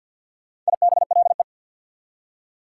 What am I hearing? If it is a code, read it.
Morse code, heard as a keyed tone, about 64 wpm, 700 Hz. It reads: IBLE